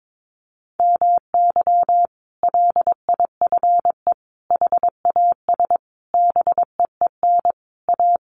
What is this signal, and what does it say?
Morse code, heard as a keyed tone, about 22 wpm, 700 Hz. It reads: MY LIFE HAS BEEN A
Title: Sentences from Top 1100 Words 22wpm